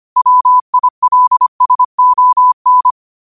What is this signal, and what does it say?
Morse code, heard as a keyed tone, about 25 wpm, 1000 Hz. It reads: WILSON
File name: qotdfiles / 25wpm